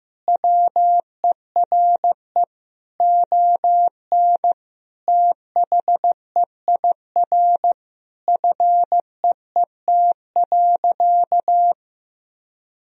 Morse code, heard as a keyed tone, about 15 wpm, 700 Hz. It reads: WERE ON THEIR FEET.
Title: Sentences from Top 1000 Words - Review Part 3 - 15wpm